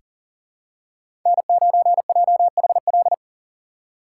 Morse code, heard as a keyed tone, about 40 wpm, 700 Hz. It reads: N9JHL
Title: K1USN SST QSO Practice - 40wpm